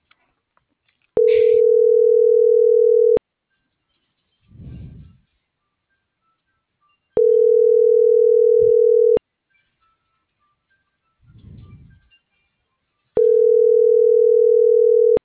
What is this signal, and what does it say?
Ringback tone (North America), 440 + 480 Hz.